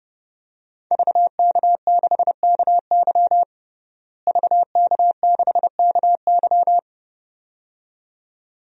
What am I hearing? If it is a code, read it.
Morse code, heard as a keyed tone, about 30 wpm, 700 Hz. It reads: VK6KY VK6KY
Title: DXCC-100 Call Signs - 5 Characters - Repeated 2x - 30wpm